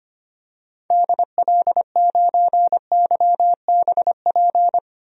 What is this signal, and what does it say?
Morse code, heard as a keyed tone, about 25 wpm, 700 Hz. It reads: DL9YBP
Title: DXCC-100 Call Signs - 6 Characters - Repeated 2x - 25wpm